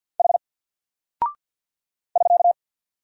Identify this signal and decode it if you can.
Morse code, heard as a keyed tone, about 48 wpm, 700 Hz. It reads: H <SK>
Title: Single Character (Letter:Number:Punctuation:Prosign) Alt 50wpm